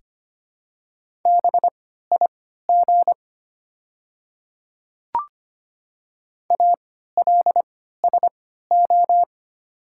Morse code, heard as a keyed tone, about 25 wpm, spaced as for 14 wpm, 700 Hz. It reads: BIG ALSO
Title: 3x Character Spacing - Top 200 Words - 25wpm